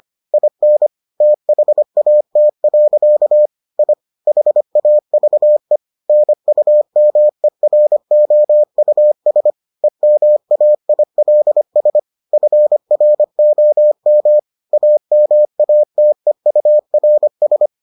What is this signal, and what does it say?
Morse code, heard as a keyed tone, about 25 wpm, 600 Hz. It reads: IN THAT. I HAVE NUMEROUS EMAILS FROM AMATEURS